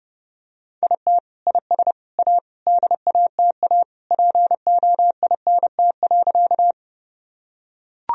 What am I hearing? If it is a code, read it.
Morse code, heard as a keyed tone, about 32 wpm, 700 Hz. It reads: IT IS A DATA POINT.
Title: Sentences from Top 1000 Words - Review Part 3 - 30wpm